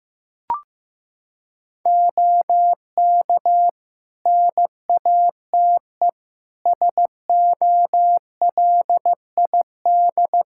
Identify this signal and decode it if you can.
Morse code, heard as a keyed tone, about 15 wpm, 700 Hz. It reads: OK NATE SOLID
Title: Practice QSO - Exchange 3 - More Info - 15wpm